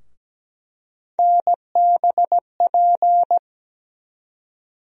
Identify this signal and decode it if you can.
Morse code, heard as a keyed tone, about 17 wpm, 700 Hz. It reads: NBP
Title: USA Call Sign Postfixes - 3 Letters 17wpm